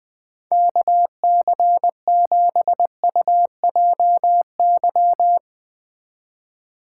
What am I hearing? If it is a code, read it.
Morse code, heard as a keyed tone, about 20 wpm, 700 Hz. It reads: KC7UJY
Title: Speed Racing - USA 2x3 Call Signs 17wpm